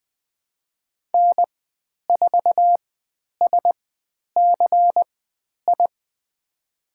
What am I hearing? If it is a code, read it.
Morse code, heard as a keyed tone, about 20 wpm, spaced as for 10 wpm, 700 Hz. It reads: N4SCI
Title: 102 Lesson 17g - Letter u - Call Signs 20wpm